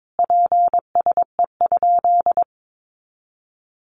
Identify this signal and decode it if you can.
Morse code, heard as a keyed tone, about 22 wpm, 700 Hz. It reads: PSE?
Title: Common QSO Phrases 22wpm